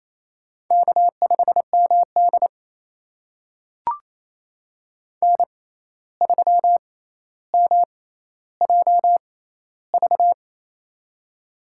Morse code, heard as a keyed tone, about 28 wpm, spaced as for 13 wpm, 700 Hz. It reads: K5MD N3MJV
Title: USA Call Signs - 6x Character Spacing - 28wpm